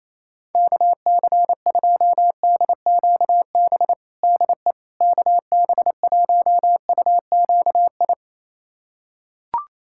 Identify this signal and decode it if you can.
Morse code, heard as a keyed tone, about 28 wpm, 700 Hz. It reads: KC2DQB DE KB1UQI
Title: USA Callsign DE Callsign 28wpm